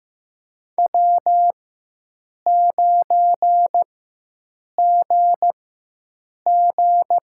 Morse code, heard as a keyed tone, about 15 wpm, spaced as for 7 wpm, 700 Hz. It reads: W9GG
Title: USA Call Signs - 5 Characters - 4x Character Spacing - 15wpm